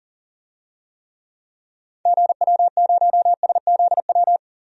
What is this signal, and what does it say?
Morse code, heard as a keyed tone, about 40 wpm, 700 Hz. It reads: GW0SZW
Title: DXCC-100 Call Signs - 6 Characters - 40wpm